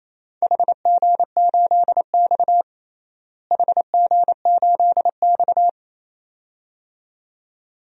Morse code, heard as a keyed tone, about 27 wpm, 700 Hz. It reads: HG8X HG8X
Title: DXCC-100 Call Signs - 4 Characters - Repeated 2x - 28wpm